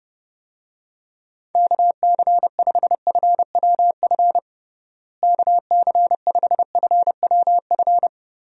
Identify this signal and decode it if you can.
Morse code, heard as a keyed tone, about 30 wpm, 700 Hz. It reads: KC5FWF KC5FWF
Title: USA Call Signs - 6 Characters - Repeated 2x - 30wpm